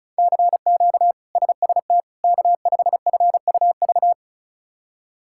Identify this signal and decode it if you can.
Morse code, heard as a keyed tone, about 36 wpm, 700 Hz. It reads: CQ SST K5FUV
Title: K1USN SST QSO Practice - 35wpm